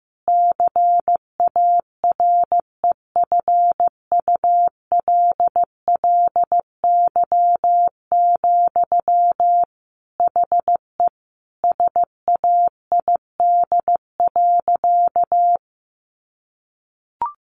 Morse code, heard as a keyed tone, about 15 wpm, 700 Hz. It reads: CAREFULLY, HE SAID.
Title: Sentences from Top 1300 Words 15wpm